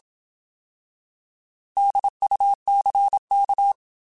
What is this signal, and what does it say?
Morse code, heard as a keyed tone, about 27 wpm, 800 Hz. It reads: DUCK